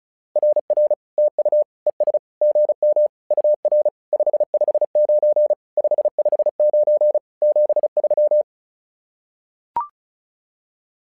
Morse code, heard as a keyed tone, about 36 wpm, 600 Hz. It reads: RR TU ES GM UR 559 559 73
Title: SOTA QSO Practice 35wpm